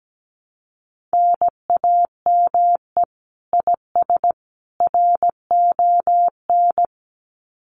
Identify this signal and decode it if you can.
Morse code, heard as a keyed tone, about 17 wpm, 700 Hz. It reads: NAME IS RON
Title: Common QSO Phrases 17wpm